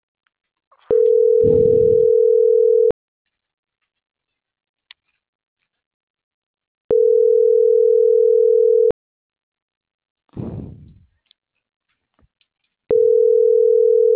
Ringback tone (North America), 440 + 480 Hz.